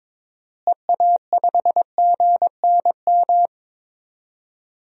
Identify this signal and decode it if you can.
Morse code, heard as a keyed tone, about 21 wpm, 700 Hz. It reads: EA5GNM
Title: DXCC-100 Call Signs - 6 Characters - Repeated 2x - 22wpm